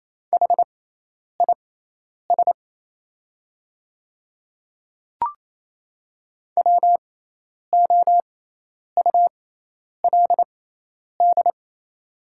Morse code, heard as a keyed tone, about 28 wpm, spaced as for 10 wpm, 690 Hz. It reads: HIS WOULD
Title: 6x Character Spacing - Top 100 Words - 28wpm